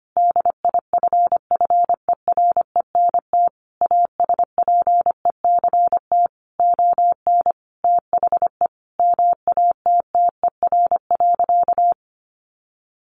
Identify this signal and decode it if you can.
Morse code, heard as a keyed tone, about 25 wpm, 700 Hz. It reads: DIFFERENT ASPECT ON THE MATTER.
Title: Sentences from Top 1300 Words 25wpm